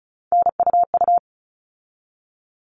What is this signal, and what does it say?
Morse code, heard as a keyed tone, about 34 wpm, 700 Hz. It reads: NUU
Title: All Cut Numbers - 3-Digit 35wpm